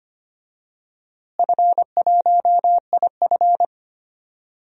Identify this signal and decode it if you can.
Morse code, heard as a keyed tone, about 25 wpm, 700 Hz. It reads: F1IF
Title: DXCC-100 Call Signs - 4 Characters - Repeated 3x - 25wpm